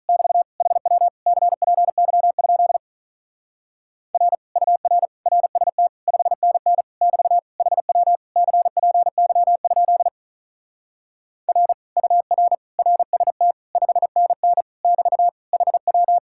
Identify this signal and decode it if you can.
Morse code, heard as a keyed tone, about 43 wpm, 700 Hz. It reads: <BT> HW CPY? R UR RST 5NN <BT> HW CPY? R UR RST 5NN <BT> HW
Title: Speed-Racing Common QSO Phrases - 35wpm